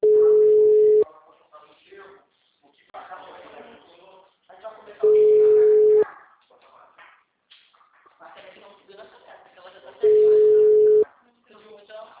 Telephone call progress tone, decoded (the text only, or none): ringback tone